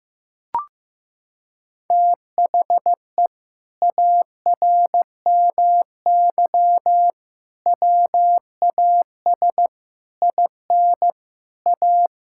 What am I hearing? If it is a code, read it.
Morse code, heard as a keyed tone, about 15 wpm, 700 Hz. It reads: THE ARMY WAS IN A
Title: Sentences from Top 1600 Words 15wpm